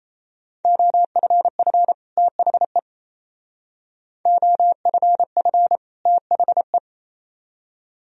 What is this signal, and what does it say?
Morse code, heard as a keyed tone, about 36 wpm, 700 Hz. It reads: OFF THE OFF THE